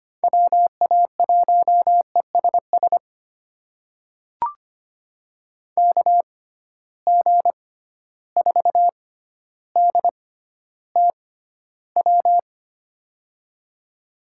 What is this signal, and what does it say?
Morse code, heard as a keyed tone, about 25 wpm, spaced as for 13 wpm, 700 Hz. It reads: WA1ESS KG4DTW
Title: USA Call Signs - 6 Characters - 6x Character Spacing - 25wpm